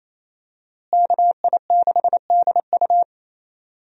Morse code, heard as a keyed tone, about 27 wpm, 700 Hz. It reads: KI6DU
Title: USA Call Signs - 5 Characters - Repeated 2x - 28wpm